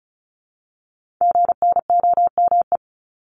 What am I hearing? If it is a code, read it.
Morse code, heard as a keyed tone, about 35 wpm, 700 Hz. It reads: GNOME
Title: Multi-letter Phonemes Repeated 3x 35wpm